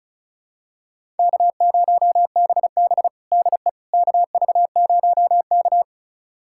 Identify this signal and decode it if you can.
Morse code, heard as a keyed tone, about 36 wpm, 700 Hz. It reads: K0BB DE KV0K
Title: USA Callsign DE Callsign - 4 Characters - 35wpm